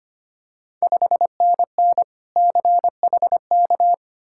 Morse code, heard as a keyed tone, about 25 wpm, 690 Hz. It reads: 5NN CHK
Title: Ontario QSO Party Exchange 25wpm